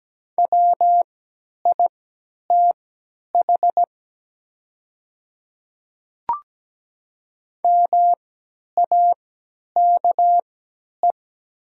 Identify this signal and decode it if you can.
Morse code, heard as a keyed tone, about 17 wpm, spaced as for 10 wpm, 690 Hz. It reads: WITH MAKE
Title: 3x Character Spacing - Top 100 Words - 17wpm